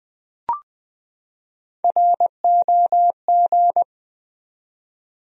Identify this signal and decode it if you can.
Morse code, heard as a keyed tone, about 21 wpm, 700 Hz. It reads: ROG
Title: USA Common Names - Max 3 Letters - 20wpm